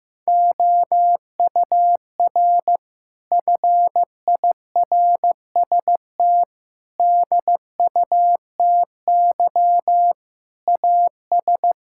Morse code, heard as a keyed tone, about 16 wpm, 700 Hz. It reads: OUR FIRST DUTY AS
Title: Sentences from Top 1600 Words 15wpm